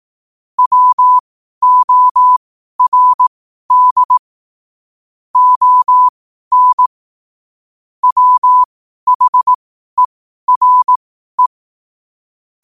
Morse code, heard as a keyed tone, about 18 wpm, spaced as for 12 wpm, 1000 Hz. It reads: WORD ON WHERE
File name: qotdfiles / 12wpm